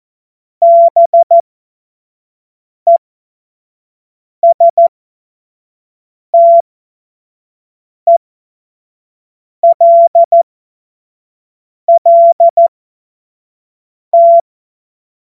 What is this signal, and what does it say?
Morse code, heard as a keyed tone, about 14 wpm, spaced as for 5 wpm, 680 Hz. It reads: BESTELLT
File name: Morsekurs / Witze / Witz 18. Mai 2021, 25 F70